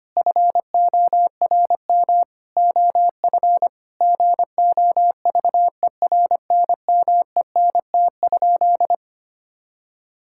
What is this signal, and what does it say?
Morse code, heard as a keyed tone, about 25 wpm, 700 Hz. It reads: FORM OF GOVERNMENT?
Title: Sentences from Top 1000 Words - Review Part 6 - 25wpm